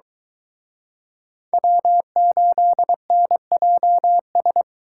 Morse code, heard as a keyed tone, about 23 wpm, 700 Hz. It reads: W8NJS